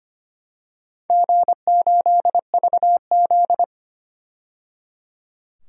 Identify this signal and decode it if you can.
Morse code, heard as a keyed tone, about 25 wpm, 700 Hz. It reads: G8VZ
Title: DXCC-100 Call Signs - 4 Characters - 25wpm